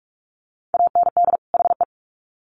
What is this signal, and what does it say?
Morse code, heard as a keyed tone, about 44 wpm, 700 Hz. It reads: AND HE
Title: Sets of 2 Words 45wpm